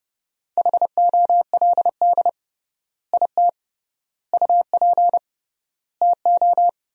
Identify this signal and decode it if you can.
Morse code, heard as a keyed tone, about 32 wpm, 700 Hz. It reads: HOLD IT UP TO
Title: 3x Word-Spacing - Sentences from Top 400 Words - 30wpm